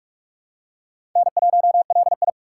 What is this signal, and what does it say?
Morse code, heard as a keyed tone, about 41 wpm, 700 Hz. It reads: N1LI
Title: USA Call Signs - 4 Characters - 4x Character Spacing - 45wpm